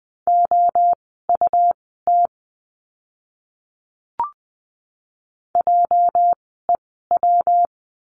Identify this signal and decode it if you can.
Morse code, heard as a keyed tone, about 20 wpm, spaced as for 15 wpm, 700 Hz. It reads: OUT JEW
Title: 206 Lesson 33f - Letter j - Words 20wpm